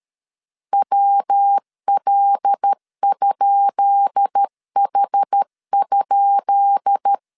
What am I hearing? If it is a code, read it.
Morse code, heard as a keyed tone, about 13 wpm, 800 Hz. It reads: WL?H?
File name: LEZIONI / 10 SIGNS / 10B INT 12wpm 2414